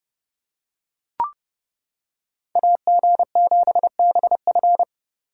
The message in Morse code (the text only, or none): AG7BF